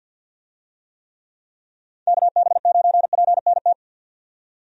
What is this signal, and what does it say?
Morse code, heard as a keyed tone, about 51 wpm, 700 Hz. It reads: KB9PNT
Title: USA Call Signs - 6 Characters - 50wpm